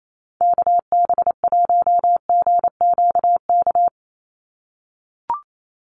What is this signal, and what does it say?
Morse code, heard as a keyed tone, about 28 wpm, 700 Hz. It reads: KB1GQK
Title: USA Call Signs 28wpm